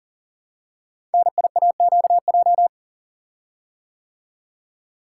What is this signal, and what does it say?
Morse code, heard as a keyed tone, about 38 wpm, 690 Hz. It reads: NIAQJ
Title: Code Groups - 5 Letters - 40wpm